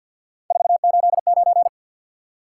Morse code, heard as a keyed tone, about 53 wpm, 700 Hz. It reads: 489
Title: Top 180 QSO Elements 50wpm